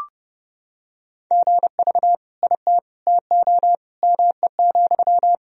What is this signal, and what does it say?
Morse code, heard as a keyed tone, about 32 wpm, 700 Hz. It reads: GV IT TO ME,
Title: Sentences from Top 200 Words and Abbreviations 30wpm